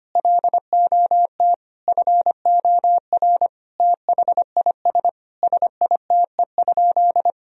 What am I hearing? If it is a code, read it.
Morse code, heard as a keyed tone, about 25 wpm, 700 Hz. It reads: LOT FOR THIS SITE?
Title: Sentences from Top 1000 Words - Review Part 4 - 25wpm